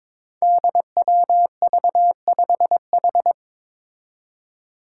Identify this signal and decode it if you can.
Morse code, heard as a keyed tone, about 22 wpm, 700 Hz. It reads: DWV5H
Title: Five Challenging Letter-Number Combinations 22wpm